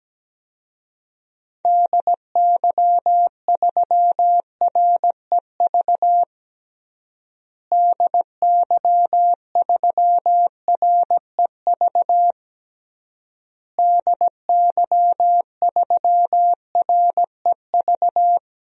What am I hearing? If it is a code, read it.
Morse code, heard as a keyed tone, about 17 wpm, 700 Hz. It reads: DY3REV DY3REV DY3REV
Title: DXCC-100 Call Signs - 6 Characters - Repeated 3x - 17wpm